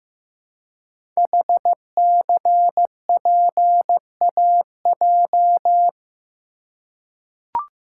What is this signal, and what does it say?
Morse code, heard as a keyed tone, about 15 wpm, 700 Hz. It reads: HCPAJ
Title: Code Groups - 5 Letters - 15wpm